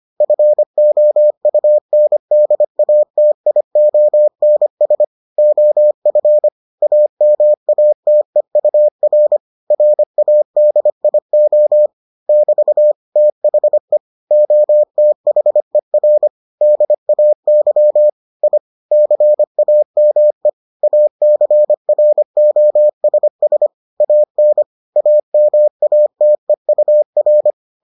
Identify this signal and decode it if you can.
Morse code, heard as a keyed tone, about 25 wpm, 600 Hz. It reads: FOUNDATIONS OF AMATEUR RADIO <BT> THE OTHER DAY I CAME ACROSS AN AMATEUR